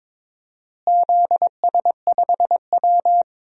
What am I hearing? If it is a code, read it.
Morse code, heard as a keyed tone, about 22 wpm, 700 Hz. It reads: ZS5W
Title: DXCC-100 Call Signs - 4 Characters - Repeated 2x - 22wpm